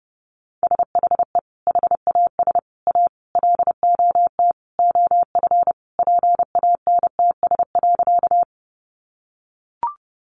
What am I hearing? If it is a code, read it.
Morse code, heard as a keyed tone, about 30 wpm, 700 Hz. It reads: SHE HAS A LOT OF PANTS.
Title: Additional Easy Phrases 30wpm